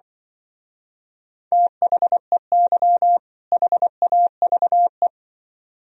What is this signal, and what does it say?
Morse code, heard as a keyed tone, about 24 wpm, 700 Hz. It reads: THEY HAVE
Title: Speed-Racing - Sets of 2 Words 20wpm